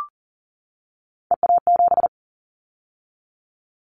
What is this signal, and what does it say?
Morse code, heard as a keyed tone, about 40 wpm, 700 Hz. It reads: EA7
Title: Callsign Prefixes for Top 25 DXCC - 40wpm